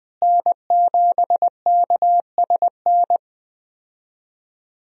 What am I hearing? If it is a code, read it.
Morse code, heard as a keyed tone, about 20 wpm, 700 Hz. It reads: N7KSN
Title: USA Call Signs - 5 Characters - Repeated 2x - 20wpm